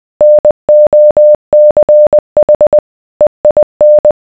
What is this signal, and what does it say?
Morse code, heard as a keyed tone, about 20 wpm, 600 Hz. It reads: NOCH EIN